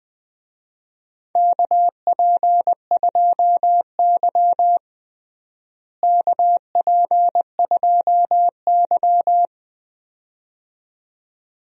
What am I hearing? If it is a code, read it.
Morse code, heard as a keyed tone, about 19 wpm, 700 Hz. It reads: KP2Y KP2Y
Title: DXCC-100 Call Signs - 4 Characters - Repeated 2x - 20wpm